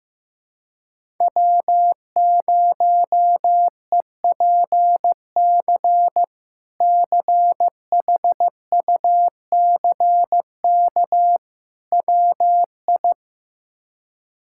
Morse code, heard as a keyed tone, about 16 wpm, 700 Hz. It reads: W0EPC CHUCK WI
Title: K1USN SST QSO Practice - 15wpm